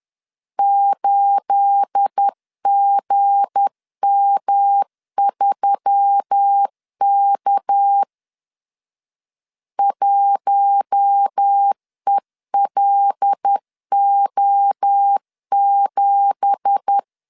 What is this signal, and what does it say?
Morse code, heard as a keyed tone, about 10 wpm, 800 Hz. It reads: 8GM3K 1ELO7
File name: LEZIONI / 9 2867 10wpm 2733